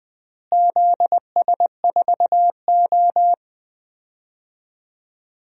Morse code, heard as a keyed tone, about 20 wpm, 700 Hz. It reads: ZS4O
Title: DXCC-100 Call Signs - 4 Characters - Repeated 3x - 20wpm